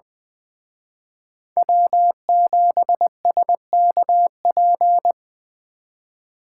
Morse code, heard as a keyed tone, about 24 wpm, 700 Hz. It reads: W7SKP
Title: Speed Racing - USA Call Signs 20wpm